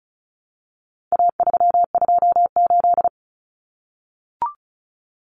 Morse code, heard as a keyed tone, about 34 wpm, 700 Hz. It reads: A328
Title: Common Cut Numbers - 4-Digit 35wpm